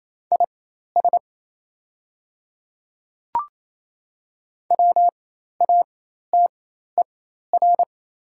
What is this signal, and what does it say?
Morse code, heard as a keyed tone, about 27 wpm, spaced as for 13 wpm, 700 Hz. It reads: IS WATER
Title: 4x Character Spacing - Top 300 Words - 28wpm